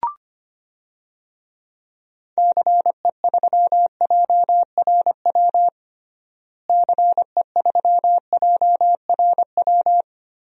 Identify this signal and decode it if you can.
Morse code, heard as a keyed tone, about 25 wpm, 700 Hz. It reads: CE3JRW CE3JRW
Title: DXCC-100 Call Signs - 6 Characters - Repeated 2x - 25wpm